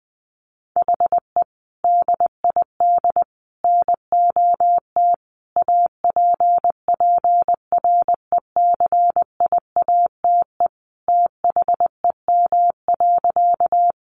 Morse code, heard as a keyed tone, about 20 wpm, 700 Hz. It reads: HE DID NOT APPRECIATE THEM.
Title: Sentences from Top 1300 Words 20wpm